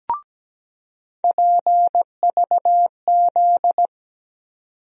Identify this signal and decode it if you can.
Morse code, heard as a keyed tone, about 17 wpm, 700 Hz. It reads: PVZ
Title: USA Call Sign Postfixes - 3 Letters 17wpm